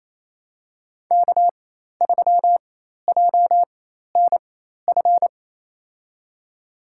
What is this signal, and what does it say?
Morse code, heard as a keyed tone, about 28 wpm, spaced as for 13 wpm, 700 Hz. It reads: K3JNF
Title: USA Call Signs - 4x Character Spacing - 28wpm